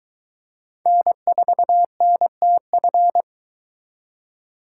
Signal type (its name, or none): Morse code